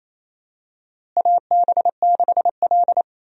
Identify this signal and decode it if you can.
Morse code, heard as a keyed tone, about 28 wpm, 700 Hz. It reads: AB6L
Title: USA Call Signs - 4 Characters - Repeated 3x - 28wpm